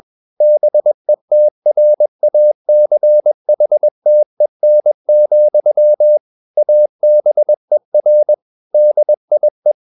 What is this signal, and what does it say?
Morse code, heard as a keyed tone, about 21 wpm, 600 Hz. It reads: BETRACHTEN, ABER DIE
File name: marienkind2 21wpm